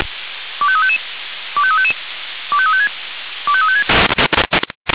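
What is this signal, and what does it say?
Five-tone selective call (ZVEI), heard as quick runs of five tones, tones of about 70 ms. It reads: 25355 25355 25356 25356
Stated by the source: ZVEI five-tone selective call received with an SDR.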